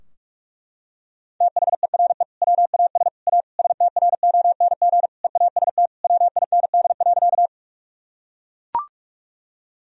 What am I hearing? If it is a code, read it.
Morse code, heard as a keyed tone, about 42 wpm, 700 Hz. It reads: THERE WAS A STRONG EAST WIND.
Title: Sentences from Top 1000 Words - Review Part 3 - 45wpm